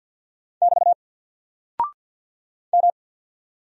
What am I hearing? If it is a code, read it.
Morse code, heard as a keyed tone, about 48 wpm, 700 Hz. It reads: <BT> M
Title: Single Character (Letter:Number:Punctuation:Prosign) Alt 50wpm